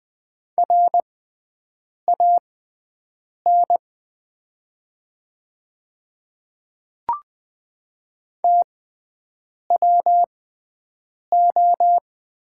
Morse code, heard as a keyed tone, about 20 wpm, spaced as for 7 wpm, 700 Hz. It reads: RAN TWO
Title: 6x Character Spacing - Top 500 Words - 20wpm